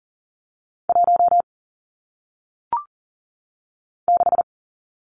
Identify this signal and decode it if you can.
Morse code, heard as a keyed tone, about 40 wpm, 700 Hz. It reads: 1 6
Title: Numbers - 1-Digit 40wpm